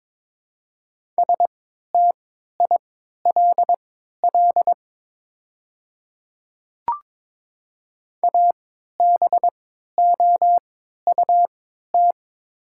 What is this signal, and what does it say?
Morse code, heard as a keyed tone, about 22 wpm, spaced as for 13 wpm, 690 Hz. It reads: STILL ABOUT
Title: 3x Character Spacing - Top 300 Words - 22wpm